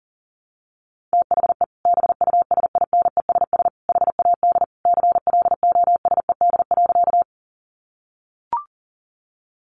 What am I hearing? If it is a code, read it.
Morse code, heard as a keyed tone, about 40 wpm, 700 Hz. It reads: THE BUSINESS HAD CLOSED.
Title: Sentences from Top 1400 Words 40wpm